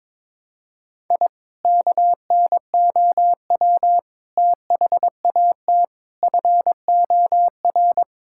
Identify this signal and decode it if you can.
Morse code, heard as a keyed tone, about 21 wpm, 700 Hz. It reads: I KNOW THAT FOR
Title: Sentences from Top 1000 Words - Review Part 5 - 22wpm